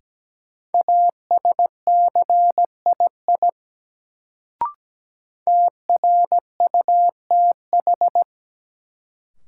Morse code, heard as a keyed tone, about 17 wpm, 700 Hz. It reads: ASCII TRUTH
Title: Sets of 5 Words - New Word Drill - 17wpm